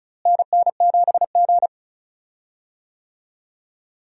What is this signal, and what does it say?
Morse code, heard as a keyed tone, about 36 wpm, 700 Hz. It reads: NN7G
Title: USA Call Signs - 4 Characters - Repeated 2x - 35wpm